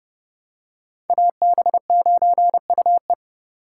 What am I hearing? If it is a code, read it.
Morse code, heard as a keyed tone, about 29 wpm, 700 Hz. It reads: AB9UE